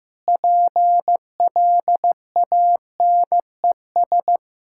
Morse code, heard as a keyed tone, about 15 wpm, 700 Hz. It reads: PLANES
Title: Pangram Challenge v2 - 15wpm